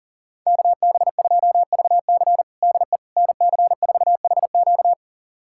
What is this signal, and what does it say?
Morse code, heard as a keyed tone, about 40 wpm, 700 Hz. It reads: KB2VC DE NC4HQ
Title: USA Callsign DE Callsign - 5 Characters - 40wpm